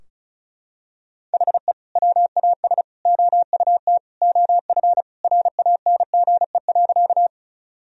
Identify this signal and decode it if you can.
Morse code, heard as a keyed tone, about 36 wpm, 700 Hz. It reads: HE WAS OUT OF RANGE.
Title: Sentences from Top 1000 Words - Review Part 4 - 35wpm